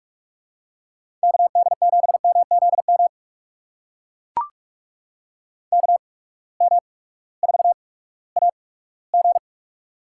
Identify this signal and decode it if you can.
Morse code, heard as a keyed tone, about 44 wpm, spaced as for 20 wpm, 700 Hz. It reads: KD7MZM KM4AG
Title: USA Call Signs - 8x Character Spacing - 45wpm